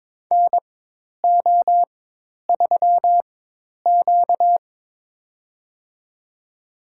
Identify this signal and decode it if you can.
Morse code, heard as a keyed tone, about 22 wpm, spaced as for 10 wpm, 700 Hz. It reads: NO3Q
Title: USA Call Signs - 4 Characters - 4x Character Spacing - 22wpm